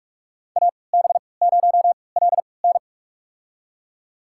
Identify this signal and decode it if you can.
Morse code, heard as a keyed tone, about 42 wpm, spaced as for 25 wpm, 700 Hz. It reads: AB0LN
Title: USA Call Signs - 5 Characters - 3x Character Spacing - 45wpm